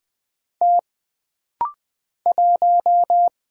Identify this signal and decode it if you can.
Morse code, heard as a keyed tone, about 20 wpm, 700 Hz. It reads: T 1
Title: Single Letter-Number Alt - Rapid-Fire 20wpm